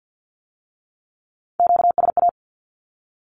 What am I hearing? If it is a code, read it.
Morse code, heard as a keyed tone, about 50 wpm, 700 Hz. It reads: QSA
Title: Q-Codes with Spoken Meaning - 50wpm